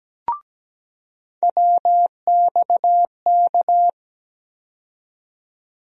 Morse code, heard as a keyed tone, about 17 wpm, 700 Hz. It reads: WXK
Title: USA Call Sign Postfixes - 3 Letters 17wpm